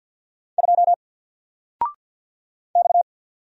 Morse code, heard as a keyed tone, about 48 wpm, 700 Hz. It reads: 2 X
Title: Single Character (Letter:Number:Punctuation:Prosign) Alt 50wpm